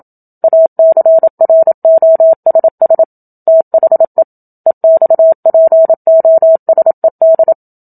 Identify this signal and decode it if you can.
Morse code, heard as a keyed tone, about 27 wpm, 650 Hz. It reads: ACROSS THE EXPOSED
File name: Book Two / 01-Chapter 1 - 27 WPM